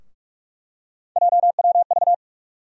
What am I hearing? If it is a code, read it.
Morse code, heard as a keyed tone, about 46 wpm, 700 Hz. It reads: JWV
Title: USA Call Sign Postfixes - 3 Letters 45wpm